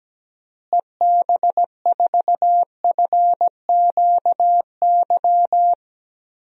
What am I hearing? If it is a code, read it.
Morse code, heard as a keyed tone, about 17 wpm, 700 Hz. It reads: EB4FQY
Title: DXCC-100 Call Signs - 6 Characters - Repeated 3x - 17wpm